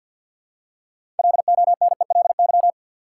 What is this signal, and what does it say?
Morse code, heard as a keyed tone, about 50 wpm, 700 Hz. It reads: LONELY
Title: Six Letter Words - 50wpm